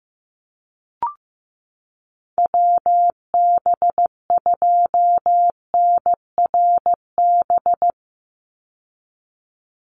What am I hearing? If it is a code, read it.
Morse code, heard as a keyed tone, about 15 wpm, 700 Hz. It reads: WB2NRB
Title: USA Call Signs 15wpm